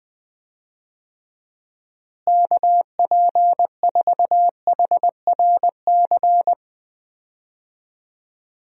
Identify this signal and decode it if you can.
Morse code, heard as a keyed tone, about 20 wpm, 700 Hz. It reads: KP4HRC
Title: USA Call Signs - 6 Characters - Repeated 2x - 20wpm